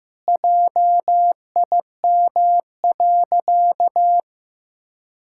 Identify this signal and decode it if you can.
Morse code, heard as a keyed tone, about 16 wpm, 700 Hz. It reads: JIM.
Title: Pangram Challenge v2 - 15wpm